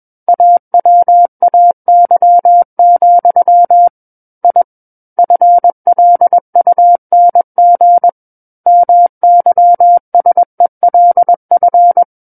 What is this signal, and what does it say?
Morse code, heard as a keyed tone, about 21 wpm, 700 Hz. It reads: AWAY, I FLUNG MYSELF
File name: Book One / 12-Chapter 12 - 21 WPM